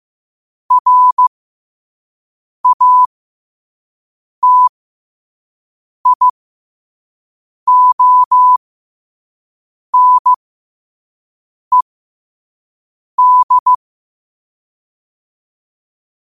Morse code, heard as a keyed tone, about 15 wpm, spaced as for 5 wpm, 1000 Hz. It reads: RATIONED
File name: qotdfiles / 5wpm